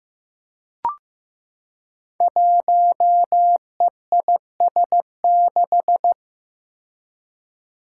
Morse code, heard as a keyed tone, about 15 wpm, 700 Hz. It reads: 1EIS6
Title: Five Challenging Letter-Number Combinations 15wpm